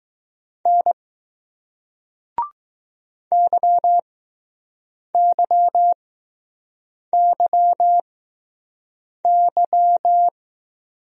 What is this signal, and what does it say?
Morse code, heard as a keyed tone, about 20 wpm, spaced as for 6 wpm, 700 Hz. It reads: N YYYY